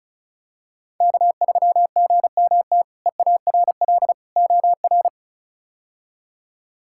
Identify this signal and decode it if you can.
Morse code, heard as a keyed tone, about 36 wpm, 700 Hz. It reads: K3GMT EARL OR
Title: K1USN SST QSO Practice - 35wpm